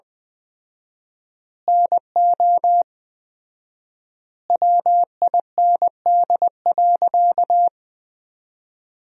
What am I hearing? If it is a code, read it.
Morse code, heard as a keyed tone, about 20 wpm, 700 Hz. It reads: NO WIND.
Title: 4x Word-Spacing - Sentences from Top 400 Words - 20wpm